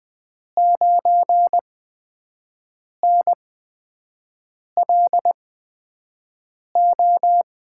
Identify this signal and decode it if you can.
Morse code, heard as a keyed tone, about 20 wpm, spaced as for 5 wpm, 700 Hz. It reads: 9NLO
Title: USA Call Signs - 6 Characters - 8x Character Spacing - 20wpm